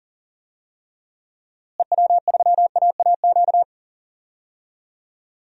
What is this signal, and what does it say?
Morse code, heard as a keyed tone, about 40 wpm, 700 Hz. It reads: EW3AAQ
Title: DXCC-100 Call Signs - 6 Characters - Repeated 2x - 40wpm